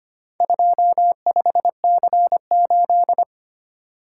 Morse code, heard as a keyed tone, about 25 wpm, 700 Hz. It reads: 25C8
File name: practice / 25wpm / Four Letter-Number Combinations - 25wpm